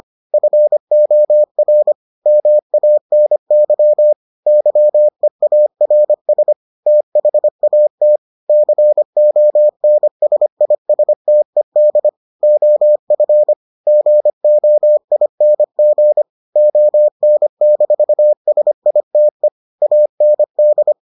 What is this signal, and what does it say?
Morse code, heard as a keyed tone, about 25 wpm, 600 Hz. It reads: FOR MANY YEARS THAT CONSISTED OF GOING ON-SITE AND